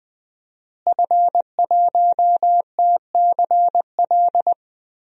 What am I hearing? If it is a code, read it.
Morse code, heard as a keyed tone, about 20 wpm, 700 Hz. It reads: F1TCL
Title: DXCC-100 Call Signs - 5 Characters - Repeated 3x - 20wpm